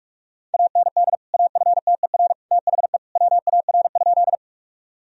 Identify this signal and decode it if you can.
Morse code, heard as a keyed tone, about 44 wpm, 700 Hz. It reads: AND AFTER THE WAR?
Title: Sentences from Top 1000 Words - Review Part 2 - 45wpm